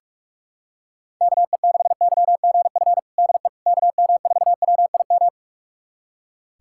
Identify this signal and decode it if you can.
Morse code, heard as a keyed tone, about 46 wpm, 700 Hz. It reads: KE6YGF DE KM4WIM
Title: USA Callsign DE Callsign - 6 Characters - 45wpm